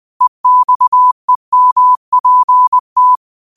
Morse code, heard as a keyed tone, about 20 wpm, 1000 Hz. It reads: EXEMPT
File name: qotdfiles / 20wpm